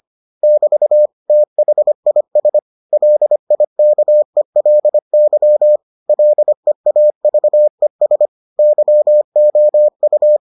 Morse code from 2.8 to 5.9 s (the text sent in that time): LIKELY